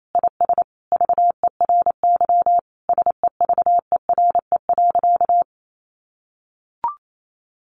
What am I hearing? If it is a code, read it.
Morse code, heard as a keyed tone, about 28 wpm, 700 Hz. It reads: IS VERY SEVERE.
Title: Sentences from Top 1500 Words 28wpm